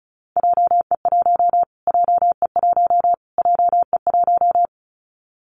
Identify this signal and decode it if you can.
Morse code, heard as a keyed tone, about 34 wpm, 700 Hz. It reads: JE1 JE1 JE1
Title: Callsign Prefixes for Top 25 DXCC - Repeated 3x - 35wpm